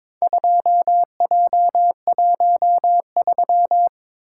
Morse code, heard as a keyed tone, about 22 wpm, 700 Hz. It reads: 2J13
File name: practice / 22wpm / Four Letter-Number Combinations - 22wpm